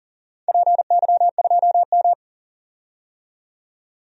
Morse code, heard as a keyed tone, about 40 wpm, 700 Hz. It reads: PY2M
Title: DXCC-100 Call Signs - 4 Characters - 40wpm